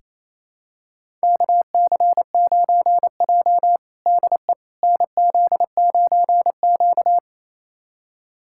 Morse code, heard as a keyed tone, about 27 wpm, 700 Hz. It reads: KC9J DE NZ9Q
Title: USA Callsign DE Callsign - 4 Characters - 28wpm